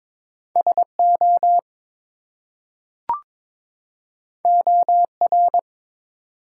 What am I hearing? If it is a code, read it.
Morse code, heard as a keyed tone, about 22 wpm, 700 Hz. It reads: SO OR
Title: Two Letter Words - 22wpm